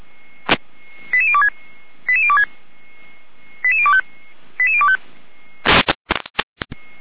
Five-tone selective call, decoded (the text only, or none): 88026 88026 88025 88025